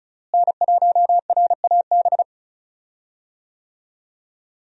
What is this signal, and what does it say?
Morse code, heard as a keyed tone, about 34 wpm, 700 Hz. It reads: N1RAB
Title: USA Call Signs - 5 Characters - Repeated 2x - 35wpm